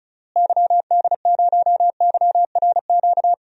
Morse code, heard as a keyed tone, about 36 wpm, 700 Hz. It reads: YD0YRQ
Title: DXCC-100 Call Signs - 6 Characters - Repeated 3x - 35wpm